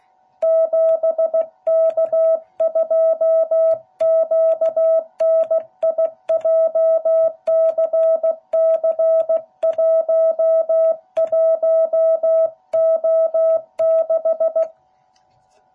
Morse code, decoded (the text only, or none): 7K2QNI JCC11O6